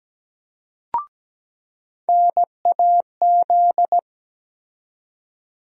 Morse code, heard as a keyed tone, about 17 wpm, 700 Hz. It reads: NAZ
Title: USA Call Sign Postfixes - 3 Letters 17wpm